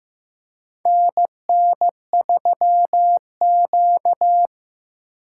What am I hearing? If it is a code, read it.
Morse code, heard as a keyed tone, about 15 wpm, 700 Hz. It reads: NN3Q
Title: USA Call Signs - 4 Characters - 8x Character Spacing - 15wpm